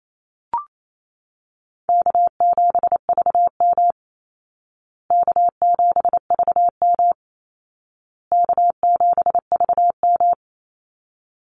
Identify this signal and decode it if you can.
Morse code, heard as a keyed tone, about 28 wpm, 700 Hz. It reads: K7VM K7VM K7VM
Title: Call Signs Repeated 2x - 28wpm